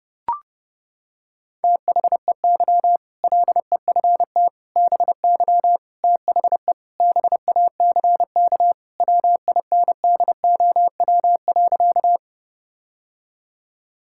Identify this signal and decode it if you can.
Morse code, heard as a keyed tone, about 29 wpm, 700 Hz. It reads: THEY LEFT BY THE BACK WINDOW.
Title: Sentences from Top 1600 Words 30wpm